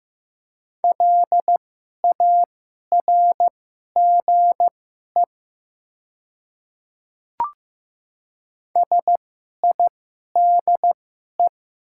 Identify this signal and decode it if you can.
Morse code, heard as a keyed tone, about 15 wpm, spaced as for 11 wpm, 690 Hz. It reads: LARGE SIDE
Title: 2x Character Spacing - Top 200 Words - 15wpm